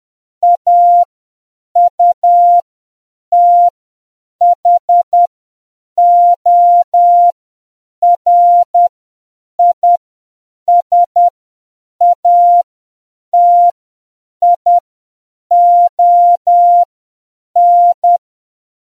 Morse code, heard as a keyed tone, about 10 wpm, spaced as for 7 wpm, 700 Hz. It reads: AUTHORISATION